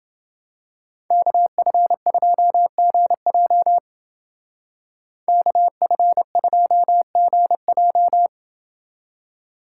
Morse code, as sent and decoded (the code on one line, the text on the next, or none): -.- ..-. ..--- --. .--- / -.- ..-. ..--- --. .---
KF2GJ KF2GJ